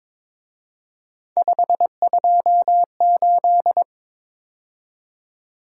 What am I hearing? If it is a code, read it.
Morse code, heard as a keyed tone, about 28 wpm, 700 Hz. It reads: 528